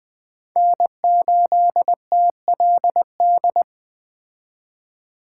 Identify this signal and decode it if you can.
Morse code, heard as a keyed tone, about 20 wpm, 700 Hz. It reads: N8TLD
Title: Speed Racing - USA 1x3 Call Signs 17wpm